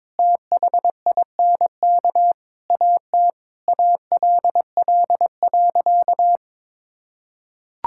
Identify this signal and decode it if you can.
Morse code, heard as a keyed tone, about 22 wpm, 700 Hz. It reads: THINK AT ALL.
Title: Sentences from Top 1600 Words 22wpm